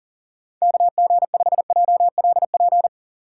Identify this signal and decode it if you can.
Morse code, heard as a keyed tone, about 40 wpm, 700 Hz. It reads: KG5JLP
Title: USA Call Signs - 6 Characters - 2x Character Spacing - 40wpm